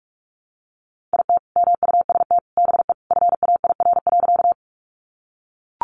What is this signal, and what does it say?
Morse code, heard as a keyed tone, about 44 wpm, 700 Hz. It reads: IT MUST BE FAIR.
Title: Sentences from Top 1300 Words 45wpm